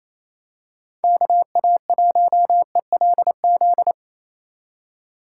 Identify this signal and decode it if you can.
Morse code, heard as a keyed tone, about 28 wpm, 700 Hz. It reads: KA1ELZ
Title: USA Call Signs - 6 Characters - 2x Character Spacing - 28wpm